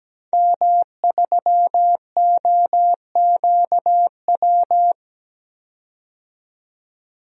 Morse code, heard as a keyed tone, about 17 wpm, 700 Hz. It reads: M3OQW
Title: DXCC-100 Call Signs - 5 Characters - 17wpm